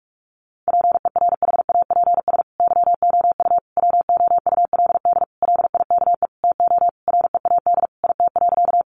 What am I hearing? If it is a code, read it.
Morse code, heard as a keyed tone, about 44 wpm, 700 Hz. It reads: PERHAPS YOU WOULD LIKE TO READ IT.
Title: Sentences from Top 500 Words 45wpm